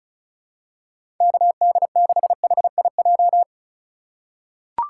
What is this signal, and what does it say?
Morse code, heard as a keyed tone, about 34 wpm, 700 Hz. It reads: KD6HIJ
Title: USA Call Signs - 6 Characters - 4x Character Spacing - 35wpm